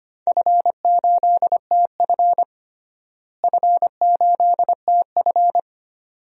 Morse code, heard as a keyed tone, about 25 wpm, 700 Hz. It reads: F8TF F8TF
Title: DXCC-100 Call Signs - 4 Characters - Repeated 2x - 25wpm